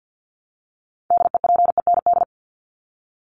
Morse code, heard as a keyed tone, about 50 wpm, 700 Hz. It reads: DEPEND
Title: Top 1001-1250 Words 50wpm